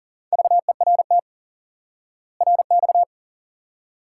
Morse code, heard as a keyed tone, about 40 wpm, 700 Hz. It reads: VERT RX
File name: IQE Course 3 - Lesson 89 - YAGI - Two Elements - 40wpm